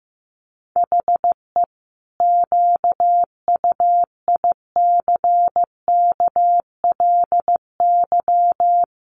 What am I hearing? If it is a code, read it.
Morse code, heard as a keyed tone, about 15 wpm, 700 Hz. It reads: HE QUICKLY
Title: Sentences from Top 700 Words 15wpm